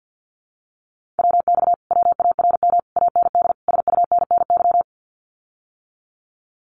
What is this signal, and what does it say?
Morse code, heard as a keyed tone, about 50 wpm, 700 Hz. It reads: WX WARM AND SUNNY
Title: Common QSO Phrases Encore 50wpm